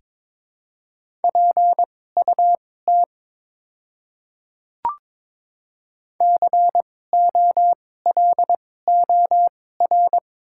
Morse code, heard as a keyed tone, about 22 wpm, spaced as for 16 wpm, 700 Hz. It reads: PUT COLOR
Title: 2x Character Spacing - Top 300 Words - 22wpm